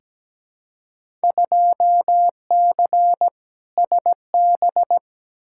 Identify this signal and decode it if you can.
Morse code, heard as a keyed tone, about 17 wpm, 700 Hz. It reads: 2C SB
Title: Field Day Exchanges Encore - 17wpm